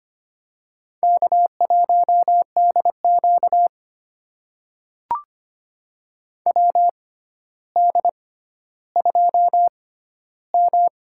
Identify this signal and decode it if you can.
Morse code, heard as a keyed tone, about 25 wpm, spaced as for 13 wpm, 700 Hz. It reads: K1DQ WD2M
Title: USA Call Signs - 4 Characters - 6x Character Spacing - 25wpm